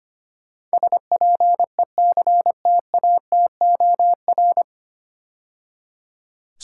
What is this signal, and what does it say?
Morse code, heard as a keyed tone, about 25 wpm, 700 Hz. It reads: SPECTATOR
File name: practice / 25wpm / 9 Letter Words - First 3 Letters Given - 25wpm